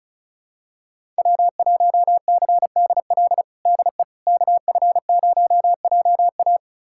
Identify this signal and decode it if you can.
Morse code, heard as a keyed tone, about 34 wpm, 700 Hz. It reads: W1CDL DE KF0JA
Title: USA Callsign DE Callsign - 5 Characters - 35wpm